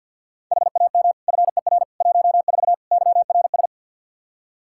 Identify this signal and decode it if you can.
Morse code, heard as a keyed tone, about 52 wpm, 700 Hz. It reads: HAM FER 14 YRS
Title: K1USN SST QSO Practice 50wpm